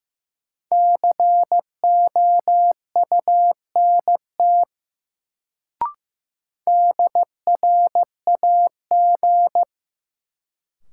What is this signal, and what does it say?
Morse code, heard as a keyed tone, about 15 wpm, 700 Hz. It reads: COUNT DRAG
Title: Sentences from Top 1500 Words - New Word Drill - 15wpm